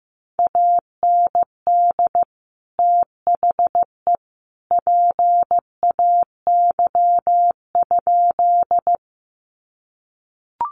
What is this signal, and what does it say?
Morse code, heard as a keyed tone, about 15 wpm, 700 Hz. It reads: AND THE PAY?
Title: Sentences from Top 700 Words 15wpm